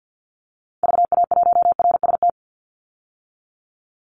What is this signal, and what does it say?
Morse code, heard as a keyed tone, about 50 wpm, 700 Hz. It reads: VA1RST
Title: Call Signs Repeated 2x - 50wpm